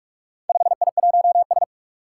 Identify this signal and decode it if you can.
Morse code, heard as a keyed tone, about 44 wpm, 700 Hz. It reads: 5I1S
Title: Four Challenging Letter-Number Combinations 45wpm